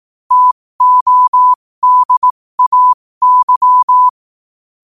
Morse code, heard as a keyed tone, about 18 wpm, spaced as for 15 wpm, 1000 Hz. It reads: TODAY
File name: qotdfiles / 15wpm